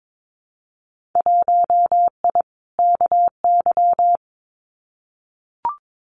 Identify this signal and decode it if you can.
Morse code, heard as a keyed tone, about 22 wpm, 700 Hz. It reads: 1I KY
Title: Winter Field Day Exchanges 22wpm